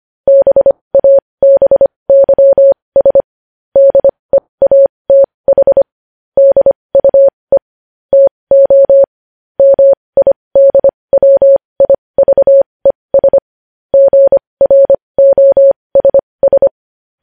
Morse code, heard as a keyed tone, about 25 wpm, spaced as for 20 wpm, 550 Hz. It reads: BABYS DEATH DUE TO MIDWIVES GROSS
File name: podcast / News headlines-1x-25x20